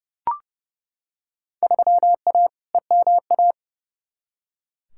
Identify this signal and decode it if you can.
Morse code, heard as a keyed tone, about 30 wpm, 700 Hz. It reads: 3A EMA
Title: Field Day Exchanges - 30wpm